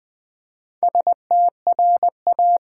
Morse code, heard as a keyed tone, about 20 wpm, 700 Hz. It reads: STRA
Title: Build A Word - 2 Letters At A Time - 8-Letter Words - 20wpm